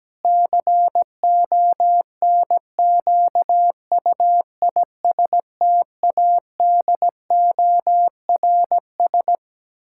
Morse code, heard as a keyed tone, about 17 wpm, 700 Hz. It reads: CONQUISTADORS
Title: Pangram Challenge v2 - 17wpm